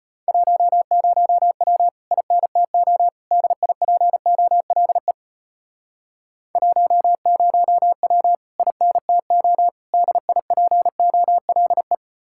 Morse code, heard as a keyed tone, about 33 wpm, 700 Hz. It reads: 10W INTO DIPOLE 10W INTO DIPOLE